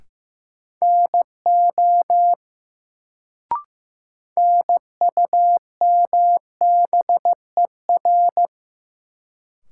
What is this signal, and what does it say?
Morse code, heard as a keyed tone, about 15 wpm, 700 Hz. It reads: NO NUMBER
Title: Sentences from Top 100 Words - New Word Drill - 15wpm